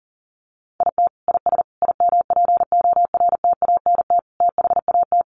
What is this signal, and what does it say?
Morse code, heard as a keyed tone, about 40 wpm, 700 Hz. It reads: IT IS IMPORTANT THAT
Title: Sentences from Top 600 Words 40wpm